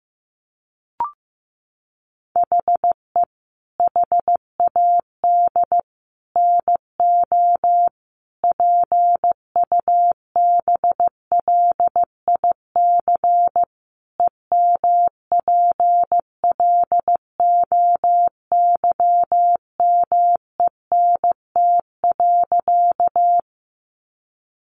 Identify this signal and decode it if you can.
Morse code, heard as a keyed tone, about 15 wpm, 700 Hz. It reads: HE HAD NO PUBLIC EMPLOYMENT.
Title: Sentences from Top 1500 Words 15wpm